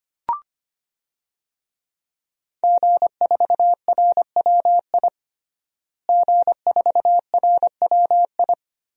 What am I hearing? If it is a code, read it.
Morse code, heard as a keyed tone, about 25 wpm, 700 Hz. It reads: G4RWI G4RWI
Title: DXCC-100 Call Signs - 5 Characters - Repeated 2x - 25wpm